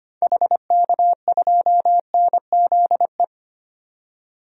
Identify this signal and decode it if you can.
Morse code, heard as a keyed tone, about 25 wpm, 700 Hz. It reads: HK2NZE
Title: DXCC-100 Call Signs - 6 Characters - 25wpm